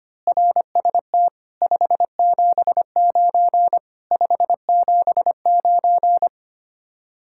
Morse code, heard as a keyed tone, about 24 wpm, 700 Hz. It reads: RST 579 579
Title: Speed-Racing Common QSO Phrases - 25wpm